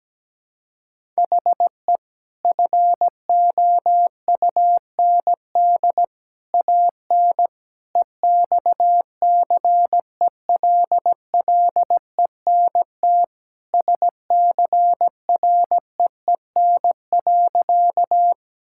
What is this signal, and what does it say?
Morse code, heard as a keyed tone, about 17 wpm, 700 Hz. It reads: HE FOUND AN EXCELLENT SCREEN.
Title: Sentences from Top 1000 Words - Review Part 6 - 17wpm